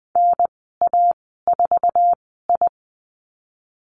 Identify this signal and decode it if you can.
Morse code, heard as a keyed tone, about 20 wpm, spaced as for 15 wpm, 700 Hz. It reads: NA4I
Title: 055 Lesson 10g - Letter r - Call Signs 20wpm